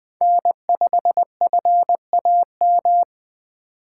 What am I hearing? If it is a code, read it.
Morse code, heard as a keyed tone, about 20 wpm, 700 Hz. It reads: N5FAM
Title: Speed Racing - USA 1x3 Call Signs 17wpm